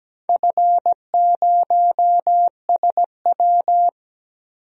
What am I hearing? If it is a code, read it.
Morse code, heard as a keyed tone, about 17 wpm, 700 Hz. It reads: F0SW
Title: DXCC-100 Call Signs - 4 Characters - Repeated 3x - 17wpm